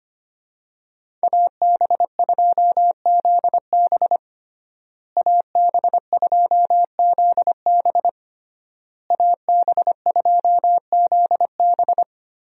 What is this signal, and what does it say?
Morse code, heard as a keyed tone, about 25 wpm, 700 Hz. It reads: AB2ZB AB2ZB AB2ZB
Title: USA Call Signs - 5 Characters - Repeated 3x - 25wpm